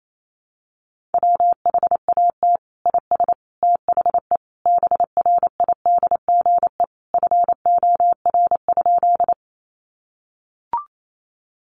Morse code, heard as a keyed tone, about 28 wpm, 700 Hz. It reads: WHAT IS THE BRIDGE FOR?
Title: Sentences from Top 1200 Words 28wpm